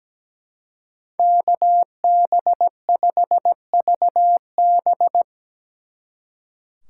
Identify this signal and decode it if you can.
Morse code, heard as a keyed tone, about 19 wpm, 700 Hz. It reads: KB5VB
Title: Speed Racing - USA 2x2 Call Signs 17wpm